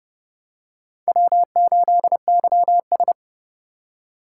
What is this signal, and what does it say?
Morse code, heard as a keyed tone, about 32 wpm, 700 Hz. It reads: W8YS
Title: USA Call Signs - 4 Characters - 4x Character Spacing - 30wpm